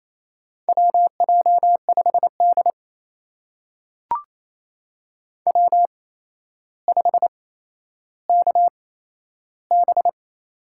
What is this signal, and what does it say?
Morse code, heard as a keyed tone, about 28 wpm, spaced as for 12 wpm, 700 Hz. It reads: WJ5D W5KB
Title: USA Call Signs - 8x Character Spacing - 28wpm